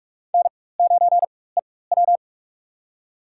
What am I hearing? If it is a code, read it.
Morse code, heard as a keyed tone, about 45 wpm, spaced as for 21 wpm, 700 Hz. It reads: N9EW
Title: USA Call Signs - 4 Characters - 4x Character Spacing - 45wpm